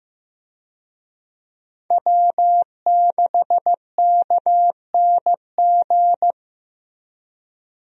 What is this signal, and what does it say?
Morse code, heard as a keyed tone, about 15 wpm, 700 Hz. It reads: W6KNG
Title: USA Call Signs - 5 Characters - 15wpm